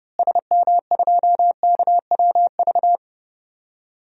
Morse code, heard as a keyed tone, about 30 wpm, 700 Hz. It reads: SM2KWV
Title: DXCC-100 Call Signs - 6 Characters - Repeated 2x - 30wpm